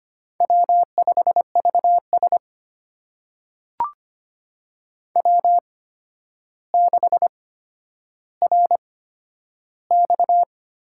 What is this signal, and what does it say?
Morse code, heard as a keyed tone, about 25 wpm, spaced as for 11 wpm, 700 Hz. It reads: W5VS W6RX
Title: USA Call Signs - 4 Characters - 8x Character Spacing - 25wpm